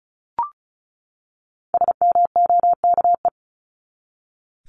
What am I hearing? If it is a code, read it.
Morse code, heard as a keyed tone, about 34 wpm, 700 Hz. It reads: SMOKE
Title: Top 1501-1750 Words 35wpm